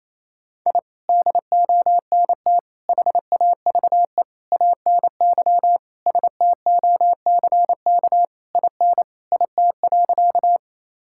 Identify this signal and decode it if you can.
Morse code, heard as a keyed tone, about 28 wpm, 700 Hz. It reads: I DONT HAVE ANY STOCK IN IT.
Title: Sentences from Top 1000 Words - Review Part 5 - 28wpm